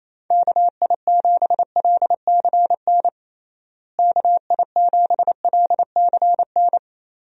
Morse code, heard as a keyed tone, about 28 wpm, 700 Hz. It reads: KI7LCN KI7LCN
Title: USA Call Signs - 6 Characters - Repeated 2x - 28wpm